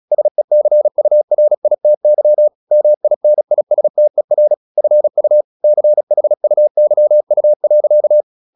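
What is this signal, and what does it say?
Morse code, heard as a keyed tone, about 36 wpm, 600 Hz. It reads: SECURITY MINISTER FU CHUYU.
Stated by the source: Morse practice at 36 wpm, text of Ken Follett's novel Never, from CW med Gnister.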